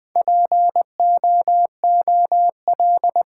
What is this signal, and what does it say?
Morse code, heard as a keyed tone, about 20 wpm, 700 Hz. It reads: POOL
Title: Four Letter Words - 20wpm